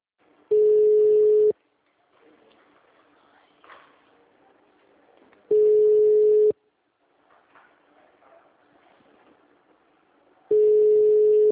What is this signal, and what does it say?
Ringback tone (Europe, 425 Hz), with speech.
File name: out- -2015- - - .29360